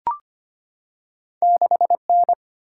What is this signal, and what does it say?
Morse code, heard as a keyed tone, about 25 wpm, 700 Hz. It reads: 6N
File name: Two Letter-Number Combinations - 25wpm